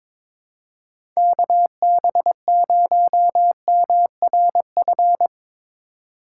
Morse code, heard as a keyed tone, about 22 wpm, 700 Hz. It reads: KB0MRF
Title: USA Call Signs - 6 Characters - 4x Character Spacing - 22wpm